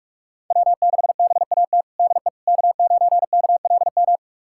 Morse code, heard as a keyed tone, about 46 wpm, 700 Hz. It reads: W6BAT DE K9KLM
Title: USA Callsign DE Callsign - 5 Characters - 45wpm